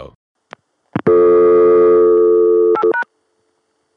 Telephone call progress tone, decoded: dial tone